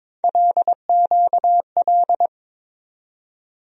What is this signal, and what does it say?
Morse code, heard as a keyed tone, about 22 wpm, 700 Hz. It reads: LQL
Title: USA Call Sign Postfixes - 3 Letters 22wpm